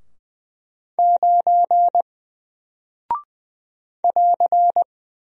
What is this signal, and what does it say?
Morse code, heard as a keyed tone, about 20 wpm, 700 Hz. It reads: 9 <AR>
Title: Single Character (Letter:Number:Punctuation:Prosign) 20wpm